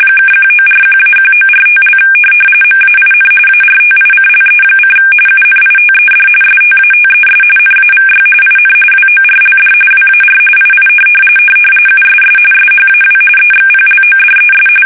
RTTY, about 45.45 baud. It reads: (1941-1945) MAIN ARTICLE: UNITED STATES NAVY IN WORLD WAR II USS SHAW EXPLODES
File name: fsk850